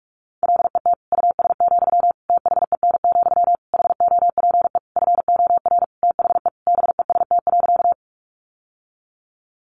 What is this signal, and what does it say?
Morse code, heard as a keyed tone, about 44 wpm, 700 Hz. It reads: LET US, THEN, HOPE FOR THE BEST.
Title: Sentences from Top 1000 Words 45wpm